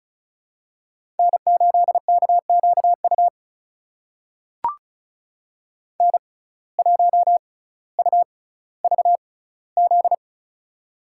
Morse code, heard as a keyed tone, about 36 wpm, spaced as for 18 wpm, 700 Hz. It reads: N8KQU N1UVZ
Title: USA Call Signs - 5 Characters - 6x Character Spacing - 35wpm